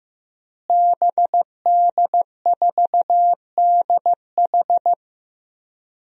Morse code, heard as a keyed tone, about 15 wpm, 700 Hz. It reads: BD4DH
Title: DXCC-100 Call Signs - 5 Characters - Repeated 2x - 15wpm